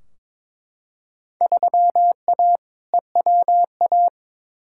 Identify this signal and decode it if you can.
Morse code, heard as a keyed tone, about 22 wpm, 700 Hz. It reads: 3A EWA
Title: Field Day Exchanges - 22wpm